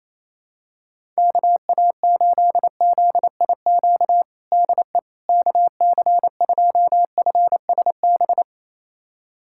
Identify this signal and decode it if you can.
Morse code, heard as a keyed tone, about 28 wpm, 700 Hz. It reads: KA8ZIQ DE KC2FSB
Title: USA Callsign DE Callsign 28wpm